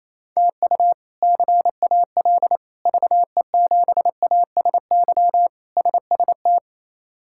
Morse code, heard as a keyed tone, about 28 wpm, 700 Hz. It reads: TU CAL VE7ASY SST
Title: K1USN SST QSO Practice - 28wpm